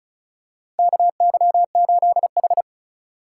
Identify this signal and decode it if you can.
Morse code, heard as a keyed tone, about 36 wpm, 700 Hz. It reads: KY8H
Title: USA Call Signs - 4 Characters - 2x Character Spacing - 35wpm